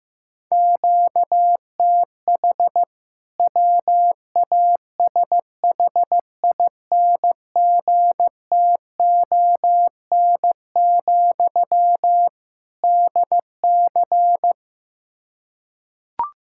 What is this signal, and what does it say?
Morse code, heard as a keyed tone, about 15 wpm, 700 Hz. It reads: QTH WASHINGTON, DC
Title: K1USN SST QSO Practice 15wpm